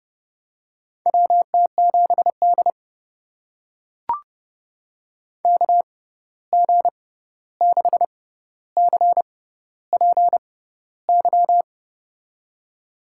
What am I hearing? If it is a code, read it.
Morse code, heard as a keyed tone, about 30 wpm, spaced as for 14 wpm, 700 Hz. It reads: WT7D KG6CPY
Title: USA Call Signs - 6x Character Spacing - 30wpm